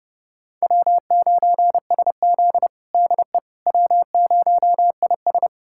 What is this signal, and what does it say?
Morse code, heard as a keyed tone, about 30 wpm, 700 Hz. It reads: W9SZ DE W0IS
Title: USA Callsign DE Callsign - 4 Characters - 30wpm